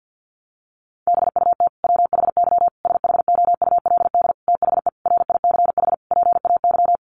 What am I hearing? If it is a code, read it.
Morse code, heard as a keyed tone, about 50 wpm, 700 Hz. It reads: BUT WHY SHOULD THE RICH PAY
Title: Sentences from Top 1100 Words 50wpm